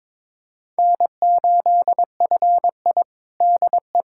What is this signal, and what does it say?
Morse code, heard as a keyed tone, about 22 wpm, 700 Hz. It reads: N8FI DE
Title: USA Callsign DE Callsign - 4 Characters - 22wpm